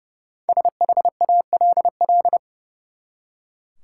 Morse code, heard as a keyed tone, about 30 wpm, 700 Hz. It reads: SHALL
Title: Five Letter Words - 30wpm